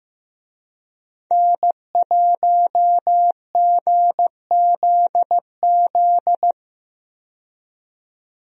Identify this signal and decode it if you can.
Morse code, heard as a keyed tone, about 15 wpm, 700 Hz. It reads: N1GZZ
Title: USA Call Signs - 5 Characters - 15wpm